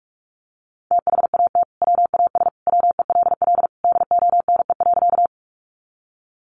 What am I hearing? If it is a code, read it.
Morse code, heard as a keyed tone, about 44 wpm, 700 Hz. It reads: THAT WAS WELL DONE.
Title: Sentences from Top 500 Words 45wpm